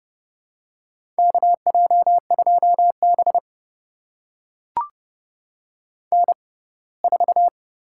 Morse code, heard as a keyed tone, about 31 wpm, spaced as for 21 wpm, 700 Hz. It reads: KJ2B N4
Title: USA Call Signs - 4 Characters - 6x Character Spacing - 30wpm